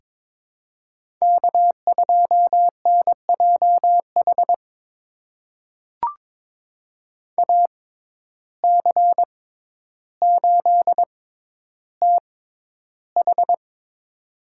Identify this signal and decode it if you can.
Morse code, heard as a keyed tone, about 23 wpm, spaced as for 11 wpm, 700 Hz. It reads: K2NJH AC8TH
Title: USA Call Signs - 5 Characters - 6x Character Spacing - 22wpm